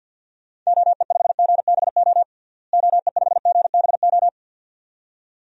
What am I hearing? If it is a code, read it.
Morse code, heard as a keyed tone, about 51 wpm, 700 Hz. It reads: OE5GBO OE5GBO
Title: DXCC-100 Call Signs - 6 Characters - Repeated 2x - 50wpm